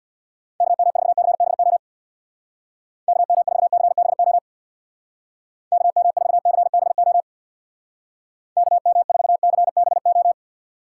Morse code, heard as a keyed tone, about 61 wpm, 700 Hz. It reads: KM4CBO KM4CBO KM4CBO KM4CBO
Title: Speed Racing - USA 2x3 Call Signs 50wpm